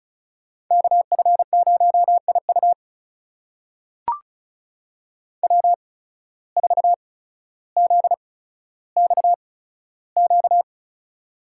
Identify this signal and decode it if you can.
Morse code, heard as a keyed tone, about 36 wpm, spaced as for 15 wpm, 700 Hz. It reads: KF0IU W4ZXQ
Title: USA Call Signs - 5 Characters - 8x Character Spacing - 35wpm